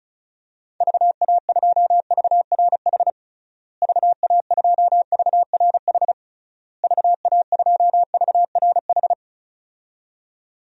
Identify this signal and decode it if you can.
Morse code, heard as a keyed tone, about 36 wpm, 700 Hz. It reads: VA2VRH VA2VRH VA2VRH
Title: DXCC-100 Call Signs - 6 Characters - Repeated 3x - 35wpm